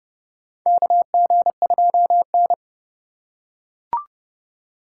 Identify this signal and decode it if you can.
Morse code, heard as a keyed tone, about 32 wpm, 700 Hz. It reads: KG2N
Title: USA Call Signs - 4x Character Spacing - 30wpm